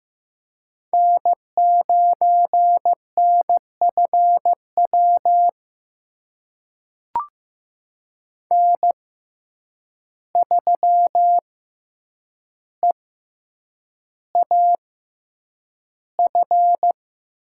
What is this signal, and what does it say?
Morse code, heard as a keyed tone, about 15 wpm, spaced as for 8 wpm, 700 Hz. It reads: N9NFW N3EAF
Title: USA Call Signs - 5 Characters - 6x Character Spacing - 15wpm